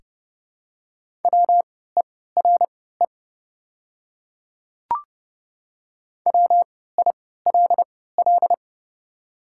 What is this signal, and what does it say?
Morse code, heard as a keyed tone, about 30 wpm, spaced as for 17 wpm, 700 Hz. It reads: WERE WILL
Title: 3x Character Spacing - Top 100 Words - 30wpm